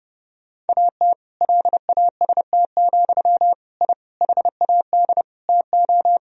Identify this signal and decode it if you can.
Morse code, heard as a keyed tone, about 30 wpm, 700 Hz. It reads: AT LAST, I HAD TO
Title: Sentences from Top 1000 Words - Review Part 2 - 30wpm